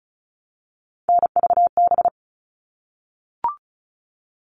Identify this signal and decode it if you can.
Morse code, heard as a keyed tone, about 34 wpm, 700 Hz. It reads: NVB
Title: All Cut Numbers - 3-Digit 35wpm